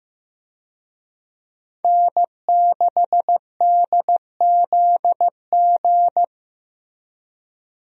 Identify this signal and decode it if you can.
Morse code, heard as a keyed tone, about 15 wpm, 700 Hz. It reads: N6DZG
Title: USA Call Signs - 5 Characters - Repeated 3x - 15wpm